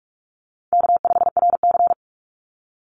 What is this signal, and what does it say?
Morse code, heard as a keyed tone, about 46 wpm, 700 Hz. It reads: K5RC
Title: Easy Call Signs - 45wpm